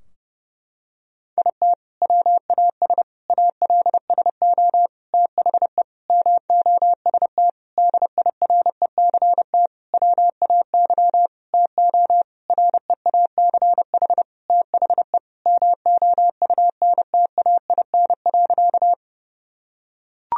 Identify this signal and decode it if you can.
Morse code, heard as a keyed tone, about 32 wpm, 700 Hz. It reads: IT WAS ALSO THE MOST DIRECT WAY TO REACH THE MOUNTAIN.
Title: Sentences from Top 1000 Words - Review Part 2 - 30wpm